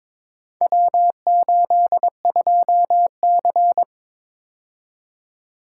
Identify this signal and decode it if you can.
Morse code, heard as a keyed tone, about 22 wpm, 700 Hz. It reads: W82C
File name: practice / 22wpm / Four Letter-Number Combinations - 22wpm